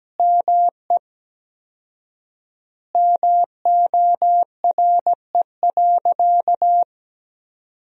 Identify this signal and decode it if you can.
Morse code, heard as a keyed tone, about 17 wpm, 700 Hz. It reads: ME MORE.
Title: 4x Word-Spacing - Sentences from Top 300 Words - 17wpm